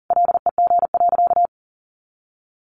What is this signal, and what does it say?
Morse code, heard as a keyed tone, about 40 wpm, 700 Hz. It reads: LEG.
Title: Additional Easy Phrases 40wpm